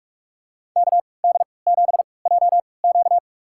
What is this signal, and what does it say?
Morse code, heard as a keyed tone, about 46 wpm, spaced as for 25 wpm, 700 Hz. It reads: KD7JQ
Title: USA Call Signs - 3x Character Spacing - 45wpm